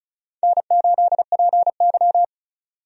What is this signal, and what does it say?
Morse code, heard as a keyed tone, about 36 wpm, 700 Hz. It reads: N8PY
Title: USA Call Signs - 4 Characters - 35wpm